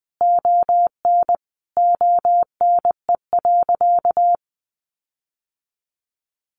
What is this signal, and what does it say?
Morse code, heard as a keyed tone, about 20 wpm, 700 Hz. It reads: ON ONE.
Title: Sentences from Top 1500 Words 20wpm